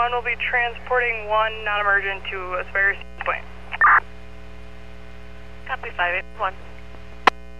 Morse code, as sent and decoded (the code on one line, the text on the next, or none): none